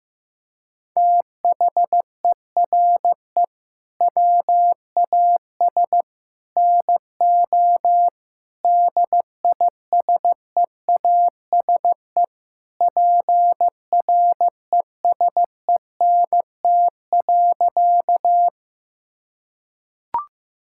Morse code, heard as a keyed tone, about 15 wpm, 700 Hz. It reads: THERE WAS NO DISEASE PRESENT.
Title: Sentences from Top 1000 Words - Review Part 6 - 15wpm